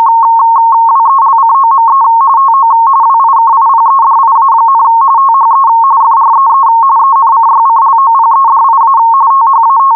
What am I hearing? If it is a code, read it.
RTTY, about 45.45 baud. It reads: WELCOME TO WIKIPEDIA, THE FREE ENCYCLOPEDIA THAT